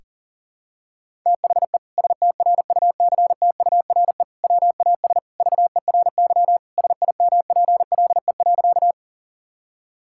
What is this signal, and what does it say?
Morse code, heard as a keyed tone, about 40 wpm, 700 Hz. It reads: THE STRUCTURE WAS VERY SIMPLE.
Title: Sentences from Top 1000 Words - Review Part 6 - 40wpm